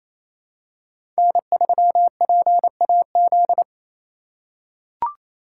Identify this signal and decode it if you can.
Morse code, heard as a keyed tone, about 27 wpm, 700 Hz. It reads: N3PAZ
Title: USA Call Signs - 3x Character Spacing - 28wpm